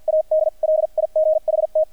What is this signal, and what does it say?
Morse code, heard as a keyed tone, about 48 wpm, spaced as for 41 wpm, 620 Hz. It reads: AD LIGHT
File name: archived / CRSnet